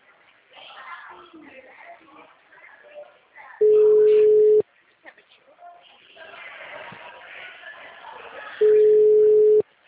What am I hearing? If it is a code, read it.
Ringback tone (Europe, 425 Hz), with speech.